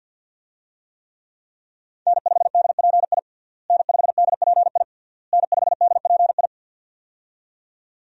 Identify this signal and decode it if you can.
Morse code, heard as a keyed tone, about 52 wpm, 700 Hz. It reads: N5DPI N5DPI N5DPI
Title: USA Call Signs - 5 Characters - Repeated 3x - 50wpm